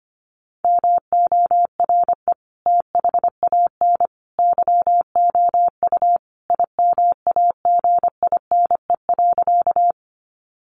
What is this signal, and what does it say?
Morse code, heard as a keyed tone, about 25 wpm, 700 Hz. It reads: MORE THAN YOU IMAGINE.
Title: Sentences from Top 1400 Words 25wpm